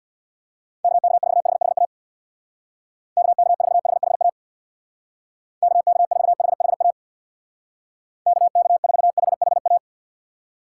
Morse code, heard as a keyed tone, about 65 wpm, 700 Hz. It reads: KK4HHA KK4HHA KK4HHA KK4HHA